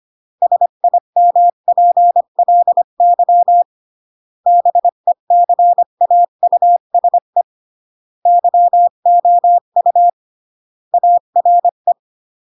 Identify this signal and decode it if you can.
Morse code, heard as a keyed tone, about 25 wpm, 700 Hz. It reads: SIMPLY BECAUSE YOU ARE